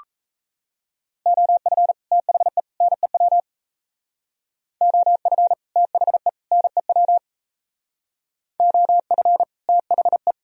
Morse code, heard as a keyed tone, about 37 wpm, 700 Hz. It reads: OF THE NEW OF THE NEW OF THE